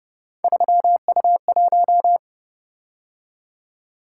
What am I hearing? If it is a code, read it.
Morse code, heard as a keyed tone, about 29 wpm, 700 Hz. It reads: 3U1